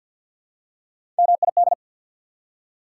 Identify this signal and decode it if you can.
Morse code, heard as a keyed tone, about 54 wpm, 690 Hz. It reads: MID
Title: Prefixes and Suffixes 50wpm